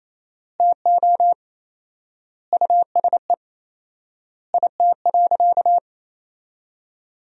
Morse code, heard as a keyed tone, about 28 wpm, 700 Hz. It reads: TO USE IT.
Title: 4x Word-Spacing - Sentences from Top 500 Words - 28wpm